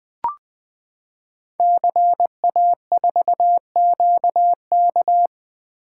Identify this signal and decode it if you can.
Morse code, heard as a keyed tone, about 20 wpm, 700 Hz. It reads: CA4QK
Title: Code Groups - 5 Letter:Numbers - 20wpm